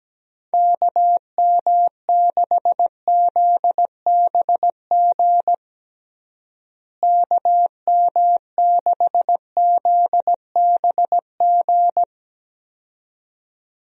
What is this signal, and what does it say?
Morse code, heard as a keyed tone, about 17 wpm, 700 Hz. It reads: KM6ZBG KM6ZBG
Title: USA Call Signs - 6 Characters - Repeated 2x - 17wpm